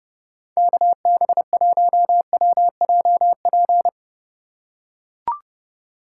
Morse code, heard as a keyed tone, about 30 wpm, 700 Hz. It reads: KB1WJP
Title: USA Call Signs - 3x Character Spacing - 30wpm